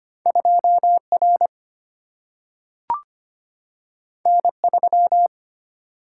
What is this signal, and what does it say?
Morse code, heard as a keyed tone, about 25 wpm, 700 Hz. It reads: 2R N3
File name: Two Letter-Number Combinations - 25wpm